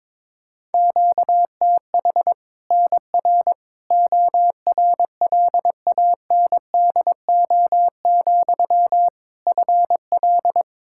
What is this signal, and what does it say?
Morse code, heard as a keyed tone, about 22 wpm, 700 Hz. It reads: QTH NR ORLANDO, FL
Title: Practice QSO - Exchange 1 - Beginning - 22wpm